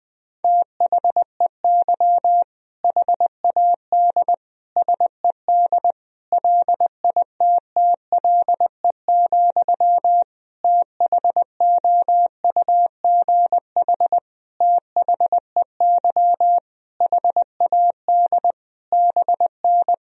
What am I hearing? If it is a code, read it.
Morse code, heard as a keyed tone, about 21 wpm, 700 Hz. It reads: THEY HAD SED LITTLE, THOUGH THEY HAD BN
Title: Sentences from Top 400 Words and Abbreviations 20wpm